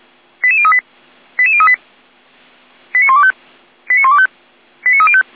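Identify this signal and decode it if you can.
Five-tone selective call (ZVEI), heard as quick runs of five tones, tones of about 70 ms. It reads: 88038 88038 89125 89125 89305
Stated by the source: ZVEI five-tone selective call 89310, received on SDR.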